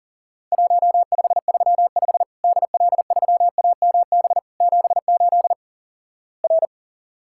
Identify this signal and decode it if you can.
Morse code, heard as a keyed tone, about 43 wpm, 700 Hz. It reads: 1535 DL3AMB 78 R
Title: Worked All Europe DX Contest - QTC Exchange - 40wpm